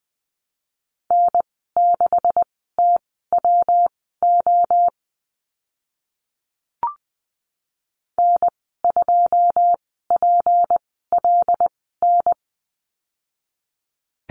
Morse code, heard as a keyed tone, about 20 wpm, spaced as for 14 wpm, 700 Hz. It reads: N6TWO N2PLN
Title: 161 Lesson 26g - Letter g - Call Signs 20wpm